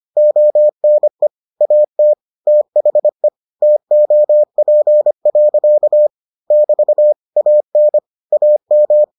Morse code, heard as a keyed tone, about 25 wpm, 600 Hz. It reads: ONE AT THE TOP. <BT> AN AM